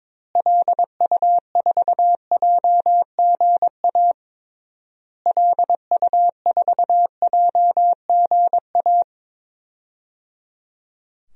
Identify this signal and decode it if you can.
Morse code, heard as a keyed tone, about 22 wpm, 700 Hz. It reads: LU4JGA LU4JGA
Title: DXCC-100 Call Signs - 6 Characters - Repeated 3x - 22wpm